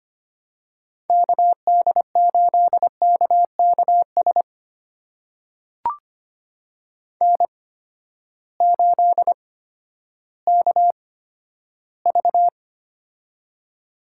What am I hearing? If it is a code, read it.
Morse code, heard as a keyed tone, about 25 wpm, spaced as for 12 wpm, 700 Hz. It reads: KD8KKS N8KV
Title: USA Call Signs - 8x Character Spacing - 25wpm